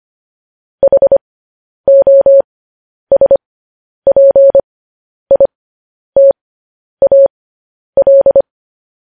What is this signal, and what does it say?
Morse code, heard as a keyed tone, about 25 wpm, spaced as for 10 wpm, 550 Hz. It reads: HOSPITAL
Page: news-headlines-in-morse-code-at-10-wpm